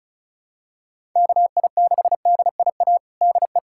Morse code, heard as a keyed tone, about 34 wpm, 700 Hz. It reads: KI6DIA DE
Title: USA Callsign DE Callsign 35wpm